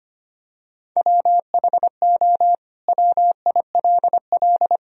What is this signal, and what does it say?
Morse code, heard as a keyed tone, about 25 wpm, 700 Hz. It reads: WHO WILL
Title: Sentences from Top 1000 Words - Review Part 3 - 25wpm